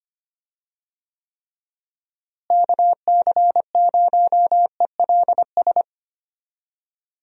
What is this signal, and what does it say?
Morse code, heard as a keyed tone, about 25 wpm, 700 Hz. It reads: KC0ELS
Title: USA Call Signs - 6 Characters - Repeated 3x - 25wpm